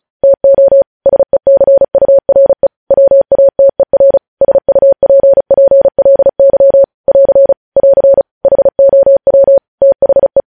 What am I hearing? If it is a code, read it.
Morse code, heard as a keyed tone, about 36 wpm, 550 Hz. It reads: TO SECURE WATER SUPPLY <AR> <AR> HOW THE
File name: podcast / News headlines-1x-35x35